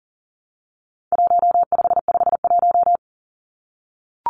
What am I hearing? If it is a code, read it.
Morse code, heard as a keyed tone, about 40 wpm, 700 Hz. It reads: 1551
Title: Numbers - Contest Style - 4-Digit 40wpm